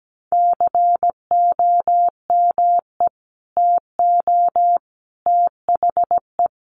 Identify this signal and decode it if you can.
Morse code, heard as a keyed tone, about 17 wpm, 700 Hz. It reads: COME TO THE
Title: Sentences from Top 1300 Words 17wpm